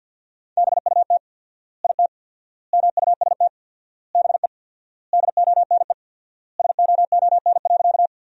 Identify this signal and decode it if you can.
Morse code, heard as a keyed tone, about 51 wpm, 700 Hz. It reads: BUT IT MUST BE DONE SOON.
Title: 4x Word-Spacing - Sentences from Top 500 Words - 50wpm